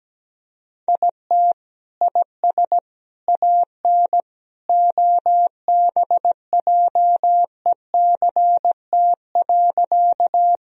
Morse code, heard as a keyed tone, about 17 wpm, 700 Hz. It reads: IT IS AN OBJECT.
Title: Sentences from Top 1000 Words - Review Part 3 - 17wpm